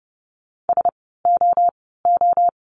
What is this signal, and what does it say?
Morse code, heard as a keyed tone, about 30 wpm, 700 Hz. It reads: S O O
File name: Lesson 16f - Letter c - Words 30wpm